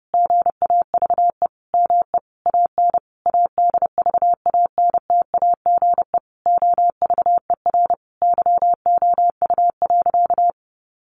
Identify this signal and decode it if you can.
Morse code, heard as a keyed tone, about 30 wpm, 700 Hz. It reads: GAVE ME AN ADVANTAGE OVER YOU.
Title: Sentences from Top 900 Words 30wpm